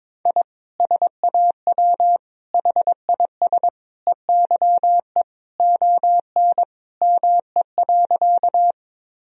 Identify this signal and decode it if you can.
Morse code, heard as a keyed tone, about 22 wpm, 700 Hz. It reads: I SAW HIS EYE ON ME.
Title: Sentences from Top 1000 Words - Review Part 1 - 22wpm